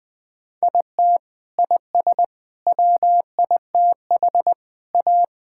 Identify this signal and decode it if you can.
Morse code, heard as a keyed tone, about 21 wpm, 700 Hz. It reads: IT IS WITH A
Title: Sentences with 8-Word Phrases - Spoken Beginning - 20wpm